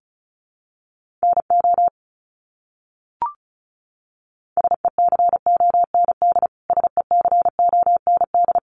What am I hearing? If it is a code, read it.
Morse code, heard as a keyed tone, about 34 wpm, 700 Hz. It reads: NO SECOND SECOND
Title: Top 500 Words Repeated 35wpm